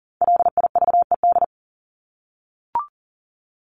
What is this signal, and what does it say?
Morse code, heard as a keyed tone, about 40 wpm, 700 Hz. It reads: LIVED
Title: ED Suffix - Top 100 Words with Base Words 40wpm